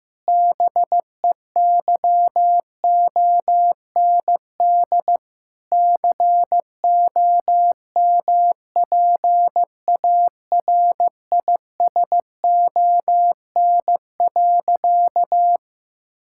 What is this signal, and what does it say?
Morse code, heard as a keyed tone, about 15 wpm, 700 Hz. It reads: BEYOND COMPARISON.
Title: Sentences from Top 1600 Words 15wpm